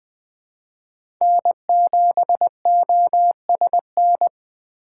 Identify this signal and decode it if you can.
Morse code, heard as a keyed tone, about 20 wpm, 700 Hz. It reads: N7OSN
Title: USA Call Signs - 5 Characters - 4x Character Spacing - 20wpm